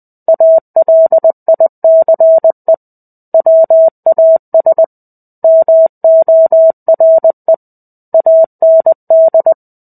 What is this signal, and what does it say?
Morse code, heard as a keyed tone, about 20 wpm, 650 Hz. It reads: ALICE WAS MORE AND
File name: 05-Chapter-05 20 WPM